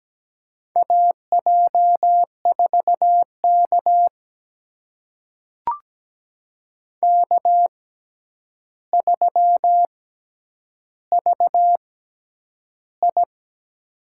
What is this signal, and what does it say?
Morse code, heard as a keyed tone, about 17 wpm, spaced as for 9 wpm, 700 Hz. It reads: AJ4K K3VI
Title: USA Call Signs - 4 Characters - 6x Character Spacing - 17wpm